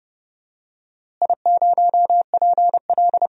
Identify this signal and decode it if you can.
Morse code, heard as a keyed tone, about 32 wpm, 700 Hz. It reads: I0PL
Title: DXCC-100 Call Signs - 4 Characters - Repeated 2x - 30wpm